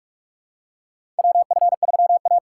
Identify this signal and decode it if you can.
Morse code, heard as a keyed tone, about 45 wpm, 700 Hz. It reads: WF3A
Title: USA Call Signs - 4 Characters - 6x Character Spacing - 45wpm